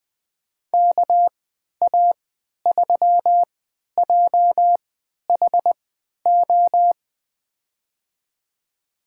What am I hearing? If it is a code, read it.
Morse code, heard as a keyed tone, about 20 wpm, spaced as for 11 wpm, 700 Hz. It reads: KA3JHO
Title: USA Call Signs - 6 Characters - 3x Character Spacing - 20wpm